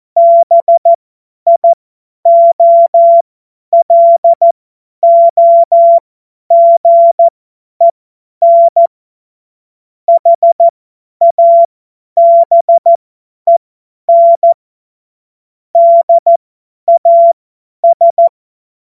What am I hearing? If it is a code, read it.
Morse code, heard as a keyed tone, about 14 wpm, spaced as for 10 wpm, 680 Hz. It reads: BIOLOGEN HABEN DAS
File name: Morsekurs / Witze / Witz 12 Dezember 2023 50 F70